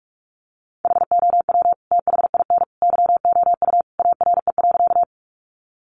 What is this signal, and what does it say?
Morse code, heard as a keyed tone, about 44 wpm, 700 Hz. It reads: HOW THIN YOU ARE.
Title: Sentences from Top 1200 Words 45wpm